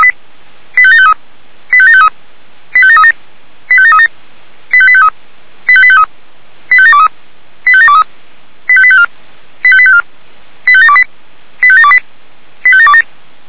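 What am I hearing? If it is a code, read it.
Five-tone selective call (ZVEI), heard as quick runs of five tones, tones of about 70 ms. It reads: 85753 85753 85737 85737 85853 85853 85723 85723 85854 85854 85728 85728 85727
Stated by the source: ZVEI five-tone call received by SDR.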